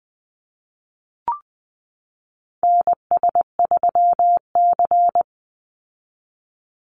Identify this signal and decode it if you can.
Morse code, heard as a keyed tone, about 20 wpm, 700 Hz. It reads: NS3C
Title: Call Signs Repeated 2x - 20wpm